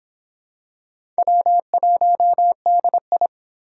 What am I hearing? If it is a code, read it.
Morse code, heard as a keyed tone, about 21 wpm, 700 Hz. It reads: W1DI